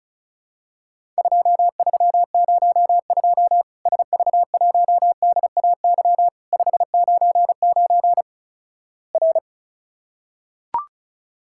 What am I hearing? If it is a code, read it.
Morse code, heard as a keyed tone, about 36 wpm, 700 Hz. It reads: 2302 SV1DAY 599 R
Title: Worked All Europe DX Contest - QTC Exchange - 35wpm